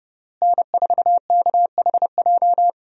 Morse code, heard as a keyed tone, about 30 wpm, 700 Hz. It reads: N4KHJ
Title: USA Call Signs - 5 Characters - Repeated 2x - 30wpm